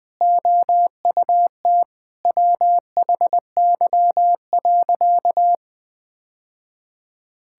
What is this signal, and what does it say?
Morse code, heard as a keyed tone, about 20 wpm, 700 Hz. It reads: OUT WHY.
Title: Sentences with 3-Word Phrases - 20wpm